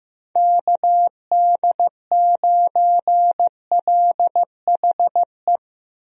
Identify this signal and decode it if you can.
Morse code, heard as a keyed tone, about 16 wpm, 700 Hz. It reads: KD9LHE
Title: USA Call Signs - 6 Characters - Repeated 2x - 15wpm